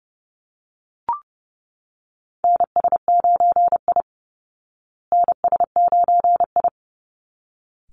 Morse code, heard as a keyed tone, about 28 wpm, 700 Hz. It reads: NS9I NS9I
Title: Call Signs Repeated 3x - 30wpm